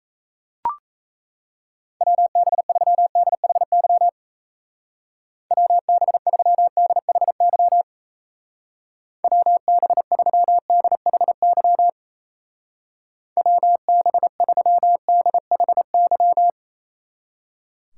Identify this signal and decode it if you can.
Morse code, heard as a keyed tone, about 35 wpm, 700 Hz. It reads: WB3DHY WB3DHY WB3DHY WB3DHY
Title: Speed Racing - USA Call Signs 28wpm